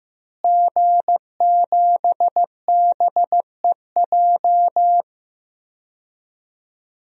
Morse code, heard as a keyed tone, about 15 wpm, 700 Hz. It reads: G7BEJ
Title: DXCC-100 Call Signs - 5 Characters - Repeated 2x - 15wpm